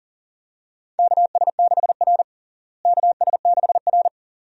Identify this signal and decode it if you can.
Morse code, heard as a keyed tone, about 40 wpm, 700 Hz. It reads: KS6R KS6R
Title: USA Call Signs - 4 Characters - Repeated 2x - 40wpm